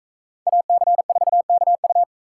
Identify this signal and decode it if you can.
Morse code, heard as a keyed tone, about 42 wpm, 700 Hz. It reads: AC4KU